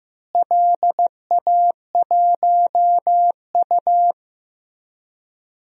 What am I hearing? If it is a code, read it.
Morse code, heard as a keyed tone, about 15 wpm, 700 Hz. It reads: LA1U
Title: DXCC-100 Call Signs - 4 Characters - Repeated 3x - 15wpm